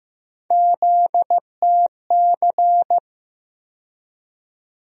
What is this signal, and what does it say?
Morse code, heard as a keyed tone, about 15 wpm, 700 Hz. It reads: ZTC
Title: USA Call Sign Postfixes - 3 Letters 15wpm